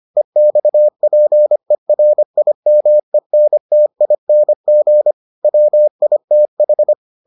Morse code, heard as a keyed tone, about 25 wpm, 600 Hz. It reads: EXPERIMENTING WITH